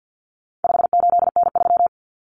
Morse code, heard as a keyed tone, about 50 wpm, 700 Hz. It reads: 58N3
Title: Common Cut Numbers - 4-Digit 50wpm